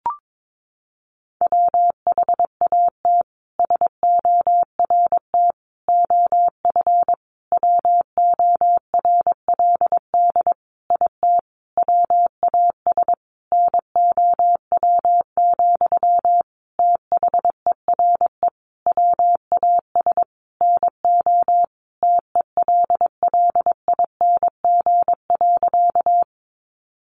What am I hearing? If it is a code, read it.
Morse code, heard as a keyed tone, about 22 wpm, 700 Hz. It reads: WHAT SORT OF WORLD IT WAS NOW, THERE WAS NO TELLING.
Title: A Sound Of Thunder 22wpm